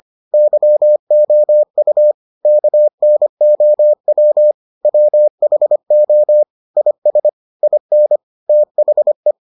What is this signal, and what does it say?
Morse code, heard as a keyed tone, about 25 wpm, 600 Hz. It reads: YOU KNOW WHO IS IN THE